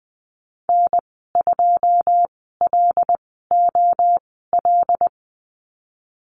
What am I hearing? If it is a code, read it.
Morse code, heard as a keyed tone, about 19 wpm, spaced as for 14 wpm, 700 Hz. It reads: N2LOL
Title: 082 Lesson 14g - Number 2 - Call Signs 20wpm